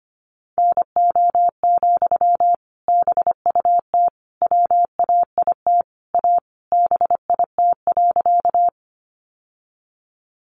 Morse code, heard as a keyed tone, about 25 wpm, 700 Hz. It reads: NO, BUT WAIT A BIT.
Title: Sentences from Top 900 Words 25wpm